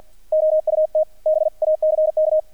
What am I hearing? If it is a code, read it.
Morse code, heard as a keyed tone, about 48 wpm, 620 Hz. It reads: OUT BACK
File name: archived / CRSnet